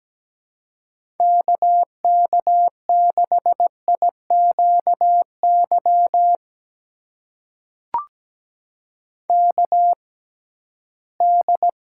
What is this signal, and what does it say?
Morse code, heard as a keyed tone, about 17 wpm, spaced as for 13 wpm, 700 Hz. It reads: KK6IQY KD
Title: USA Call Signs - 6 Characters - 6x Character Spacing - 17wpm